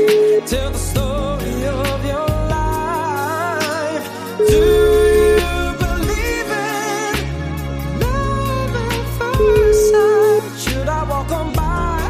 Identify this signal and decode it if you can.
Ringback tone (Europe, 425 Hz), with noise.